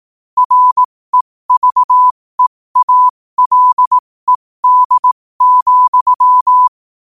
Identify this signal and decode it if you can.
Morse code, heard as a keyed tone, about 18 wpm, spaced as for 15 wpm, 1000 Hz. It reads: REVEALED,
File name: qotdfiles / 15wpm